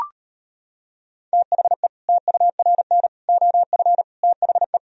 Morse code, heard as a keyed tone, about 33 wpm, 700 Hz. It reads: THE TURN OF THE